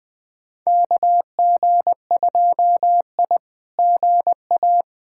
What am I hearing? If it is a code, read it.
Morse code, heard as a keyed tone, about 20 wpm, 700 Hz. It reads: KG2I GA
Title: SOTA QSO Practice 20wpm